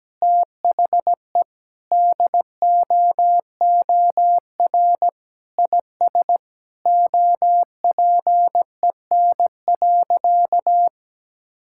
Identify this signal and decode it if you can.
Morse code, heard as a keyed tone, about 17 wpm, 700 Hz. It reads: THE DOOR IS OPEN.
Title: Sentences from Top 1000 Words - Review Part 6 - 17wpm